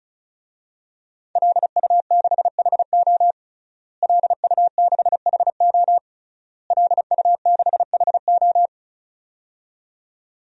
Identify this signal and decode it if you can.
Morse code, heard as a keyed tone, about 34 wpm, 700 Hz. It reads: LU6HO LU6HO LU6HO
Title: DXCC-100 Call Signs - 5 Characters - Repeated 3x - 35wpm